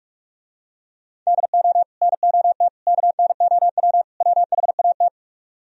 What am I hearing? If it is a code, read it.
Morse code, heard as a keyed tone, about 46 wpm, 700 Hz. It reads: DO NOT KNOW WHAT